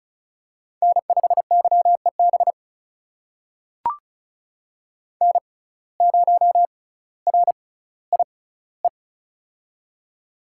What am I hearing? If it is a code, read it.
Morse code, heard as a keyed tone, about 36 wpm, spaced as for 18 wpm, 700 Hz. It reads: N5YEB N0RIE
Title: USA Call Signs - 5 Characters - 6x Character Spacing - 35wpm